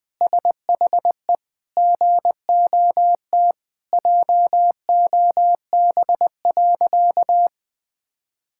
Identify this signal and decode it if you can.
Morse code, heard as a keyed tone, about 20 wpm, 700 Hz. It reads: SHE GOT JOB.
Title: Sentences with 4-Word Phrases - Spoken Beginning - 20wpm